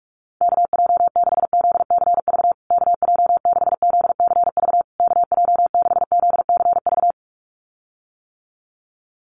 Morse code, heard as a keyed tone, about 44 wpm, 700 Hz. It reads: KJ6ZCV KJ6ZCV KJ6ZCV
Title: USA Call Signs Repeated 3x 45wpm